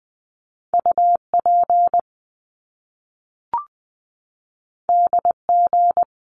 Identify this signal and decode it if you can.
Morse code, heard as a keyed tone, about 20 wpm, 700 Hz. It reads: UP DG
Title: Top Two Letter Combinations in Words 20wpm